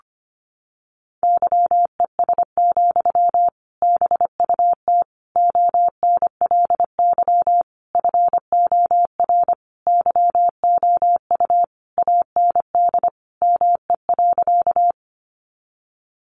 Morse code, heard as a keyed tone, about 25 wpm, 700 Hz. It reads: YES, BUT ONLY FOR YOU AND ME.
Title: Sentences from Top 500 Words 25wpm